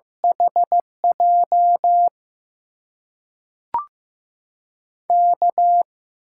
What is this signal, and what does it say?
Morse code, heard as a keyed tone, about 15 wpm, 700 Hz. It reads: HJ K
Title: USA Call Signs - 6 Characters - 8x Character Spacing - 15wpm